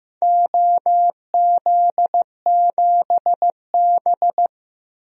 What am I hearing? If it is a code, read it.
Morse code, heard as a keyed tone, about 15 wpm, 700 Hz. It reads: OZ7B
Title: DXCC-100 Call Signs - 4 Characters - Repeated 3x - 15wpm